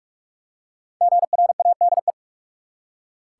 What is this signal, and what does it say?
Morse code, heard as a keyed tone, about 46 wpm, 700 Hz. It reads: GRADE
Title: Five Letter Words - 45wpm